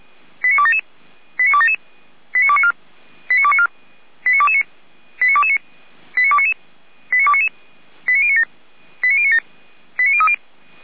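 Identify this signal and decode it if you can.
Five-tone selective call (ZVEI), heard as quick runs of five tones, tones of about 70 ms. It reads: 89277 89277 89294 89294 89229 89229 89299 89299 89097 89097 89030
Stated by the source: ZVEI five-tone selective call 89310 received via SDR.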